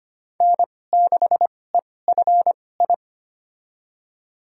Morse code, heard as a keyed tone, about 25 wpm, spaced as for 18 wpm, 700 Hz. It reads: N6EFI
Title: USA Call Signs - 5 Characters - 2x Character Spacing - 25wpm